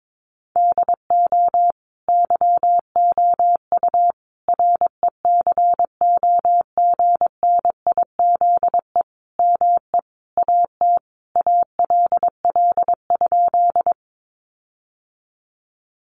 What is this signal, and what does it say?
Morse code, heard as a keyed tone, about 22 wpm, 700 Hz. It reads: DO YOU RECOGNIZE ME AT ALL?
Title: Sentences from Top 1400 Words 22wpm